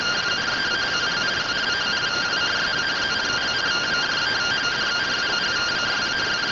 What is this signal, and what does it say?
Two-tone teleprinter signal, about 170 Hz shift.